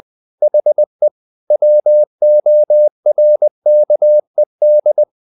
Morse code, heard as a keyed tone, about 20 wpm, 600 Hz. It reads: HE WORKED
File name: the old man and the sea / the old man and the sea part 1 20wpm